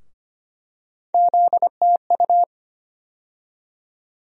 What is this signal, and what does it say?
Morse code, heard as a keyed tone, about 25 wpm, 700 Hz. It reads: ZTU
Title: USA Call Sign Postfixes - 3 Letters 25wpm